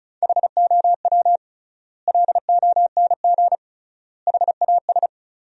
Morse code, heard as a keyed tone, about 36 wpm, 700 Hz. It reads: HOW LONG HAS
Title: 3x Word-Spacing - Sentences from Top 100 Words - 35wpm